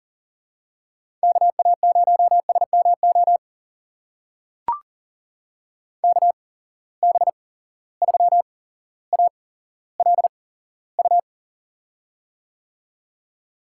Morse code, heard as a keyed tone, about 40 wpm, spaced as for 17 wpm, 700 Hz. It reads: KA0SMO KB3ALU
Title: USA Call Signs - 6 Characters - 8x Character Spacing - 40wpm